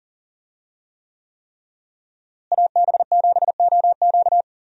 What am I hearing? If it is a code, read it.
Morse code, heard as a keyed tone, about 40 wpm, 700 Hz. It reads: AB7OQ
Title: USA Call Signs - 5 Characters - Repeated 3x - 40wpm